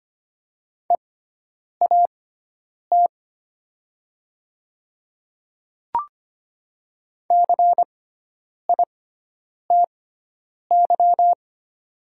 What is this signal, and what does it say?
Morse code, heard as a keyed tone, about 25 wpm, spaced as for 9 wpm, 700 Hz. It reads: EAT CITY
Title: 6x Character Spacing - Top 300 Words - 25wpm